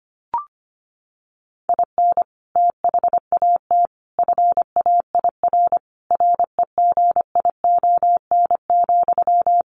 Morse code, heard as a keyed tone, about 25 wpm, 700 Hz. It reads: IN THAT FAIR REGION,
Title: Sentences from Top 1500 Words 25wpm